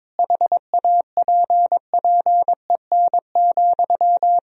Morse code, heard as a keyed tone, about 22 wpm, 700 Hz. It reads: HAPPEN,
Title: Sentences from Top 1000 Words - Review Part 4 - 22wpm